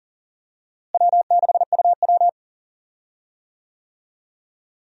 Morse code, heard as a keyed tone, about 39 wpm, 700 Hz. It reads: W6UW
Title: USA Call Signs - 4 Characters - Repeated 3x - 40wpm